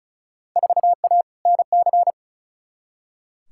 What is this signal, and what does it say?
Morse code, heard as a keyed tone, about 34 wpm, 700 Hz. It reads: 4A NC
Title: Field Day Exchanges - 35wpm